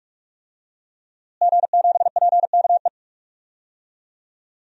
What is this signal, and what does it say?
Morse code, heard as a keyed tone, about 41 wpm, 700 Hz. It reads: G7PKE
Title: DXCC-100 Call Signs - 5 Characters - Repeated 3x - 45wpm